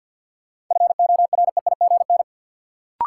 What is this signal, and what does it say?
Morse code, heard as a keyed tone, about 52 wpm, 700 Hz. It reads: FOREIGN
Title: Sentences from Top 1400 Words - New Word Drill - 50wpm